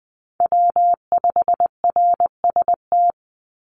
Morse code, heard as a keyed tone, about 20 wpm, 700 Hz. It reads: W5RST
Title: NA SOTA Chaser Callsigns 20wpm